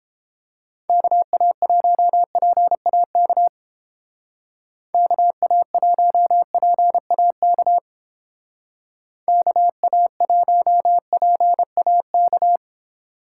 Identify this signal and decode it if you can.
Morse code, heard as a keyed tone, about 30 wpm, 700 Hz. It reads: KA1PAK KA1PAK KA1PAK